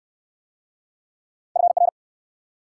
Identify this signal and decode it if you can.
Morse code, heard as a keyed tone, about 69 wpm, 690 Hz. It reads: FR